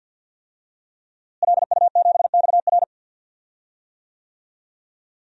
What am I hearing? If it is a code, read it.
Morse code, heard as a keyed tone, about 53 wpm, 700 Hz. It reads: LU7XR
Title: DXCC-100 Call Signs - 5 Characters - Repeated 3x - 50wpm